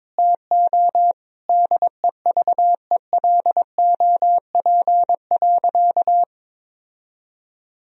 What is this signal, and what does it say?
Morse code, heard as a keyed tone, about 22 wpm, 700 Hz. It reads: TO DEVELOP.
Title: Sentences from Top 1000 Words - Review Part 3 - 22wpm